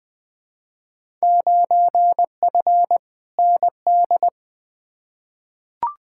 Morse code, heard as a keyed tone, about 21 wpm, 700 Hz. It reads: 9F ND
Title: Field Day Exchanges Encore - 20wpm